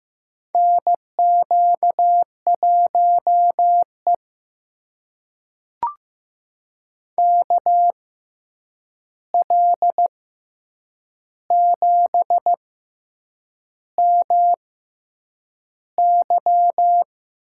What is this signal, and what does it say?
Morse code, heard as a keyed tone, about 15 wpm, spaced as for 7 wpm, 700 Hz. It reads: NQ1E KL7MY
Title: USA Call Signs - 6x Character Spacing - 15wpm